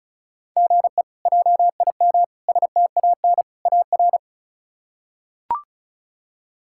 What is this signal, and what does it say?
Morse code, heard as a keyed tone, about 36 wpm, 700 Hz. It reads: GE JIM STAN AR
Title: K1USN SST QSO Practice - 35wpm